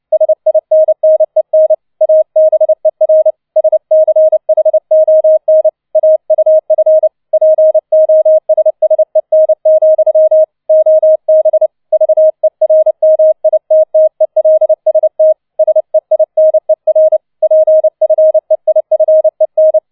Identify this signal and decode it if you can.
Morse code, heard as a keyed tone, about 29 wpm, 630 Hz. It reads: SINNEN ABER SCHON AUF POSSEN, OB VERMITTELST SEINER PFEIFEN